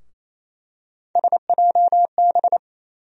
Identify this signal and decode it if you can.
Morse code, heard as a keyed tone, about 28 wpm, 700 Hz. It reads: SJB
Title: USA Call Sign Postfixes - 3 Letters 28wpm